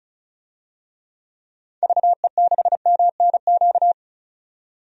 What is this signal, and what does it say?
Morse code, heard as a keyed tone, about 34 wpm, 700 Hz. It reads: VE6MNQ
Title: DXCC-100 Call Signs - 6 Characters - 35wpm